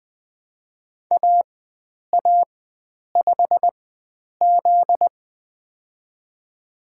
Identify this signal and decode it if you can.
Morse code, heard as a keyed tone, about 20 wpm, spaced as for 9 wpm, 700 Hz. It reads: AA5Z
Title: USA Call Signs - 4 Characters - 4x Character Spacing - 20wpm